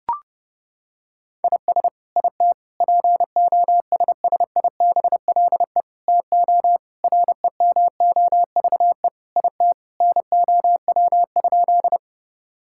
Morse code, heard as a keyed tone, about 30 wpm, 700 Hz. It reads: IS IT POSSIBLE TO REMOVE IT NOW?
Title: Sentences from Top 1000 Words - Review Part 5 - 30wpm